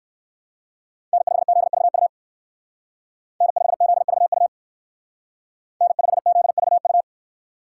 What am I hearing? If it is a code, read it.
Morse code, heard as a keyed tone, about 54 wpm, 700 Hz. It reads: N5ZVU N5ZVU N5ZVU
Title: Speed Racing - USA 1x3 Call Signs 45wpm